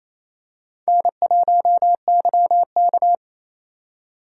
Morse code, heard as a keyed tone, about 28 wpm, 700 Hz. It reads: N1YK
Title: USA Call Signs - 4 Characters - 4x Character Spacing - 28wpm